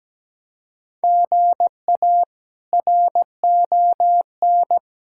Morse code, heard as a keyed tone, about 17 wpm, 700 Hz. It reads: GA RON
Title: K1USN SST QSO Practice - 17wpm